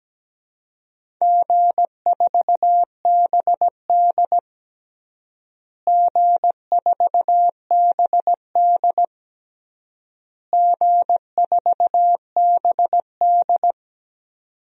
Morse code, heard as a keyed tone, about 17 wpm, 700 Hz. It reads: G4BD G4BD G4BD
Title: DXCC-100 Call Signs - 4 Characters - Repeated 3x - 17wpm